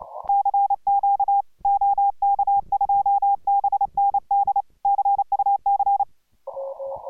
Morse code, heard as a keyed tone, about 29 wpm, 800 Hz. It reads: CQ OK2BND CUC